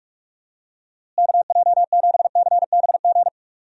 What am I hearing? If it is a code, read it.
Morse code, heard as a keyed tone, about 45 wpm, 700 Hz. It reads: KJ7CBG
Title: USA Call Signs - 6 Characters - Repeated 2x - 45wpm